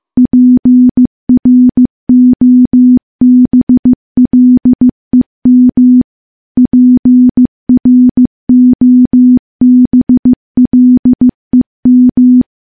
Morse code, heard as a keyed tone, about 15 wpm, 250 Hz. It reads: PROBLEM PROBLEM
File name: morse / morse-training-15WPM